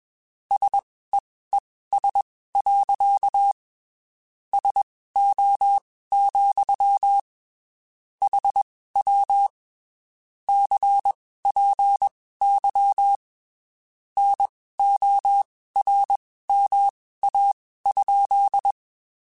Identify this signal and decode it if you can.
Morse code, heard as a keyed tone, about 21 wpm, spaced as for 15 wpm, 800 Hz. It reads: SEES. SO, HW CPY NORMA?